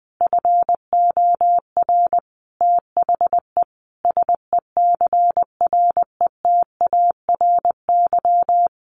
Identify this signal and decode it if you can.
Morse code, heard as a keyed tone, about 19 wpm, 700 Hz. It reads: FOR THE SECRETARY
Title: Sentences from Top 1400 Words 20wpm